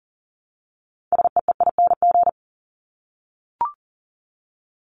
Morse code, heard as a keyed tone, about 40 wpm, 700 Hz. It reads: SEEING
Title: ING Suffix - Top 100 Words with Base Words 40wpm